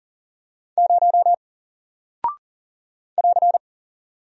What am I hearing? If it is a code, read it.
Morse code, heard as a keyed tone, about 40 wpm, 700 Hz. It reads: 0 <AR>
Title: Single Character (Letter:Number:Punctuation:Prosign) Alt 40wpm